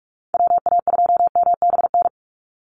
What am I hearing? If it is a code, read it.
Morse code, heard as a keyed tone, about 46 wpm, 700 Hz. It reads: WA2MBN
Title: USA Call Signs 45wpm